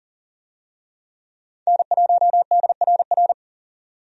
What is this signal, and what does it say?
Morse code, heard as a keyed tone, about 37 wpm, 700 Hz. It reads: N1DRR
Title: USA Call Signs - 5 Characters - 40wpm